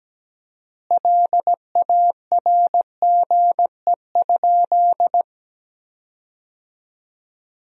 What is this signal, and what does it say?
Morse code, heard as a keyed tone, about 17 wpm, 700 Hz. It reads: LARGE?
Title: 4x Word-Spacing - Sentences from Top 200 Words - 17wpm